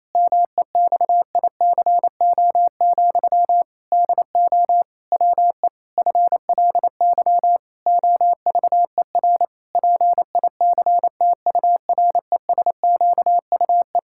Morse code, heard as a keyed tone, about 28 wpm, 700 Hz. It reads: MEXICO, DO WE FLY OVER PICTURESQUE
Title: Pangram Challenge v2 - 28wpm